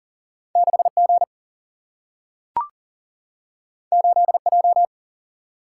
Morse code, heard as a keyed tone, about 40 wpm, 700 Hz. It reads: 6G 8J
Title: Single-Letter Single-Number 40wpm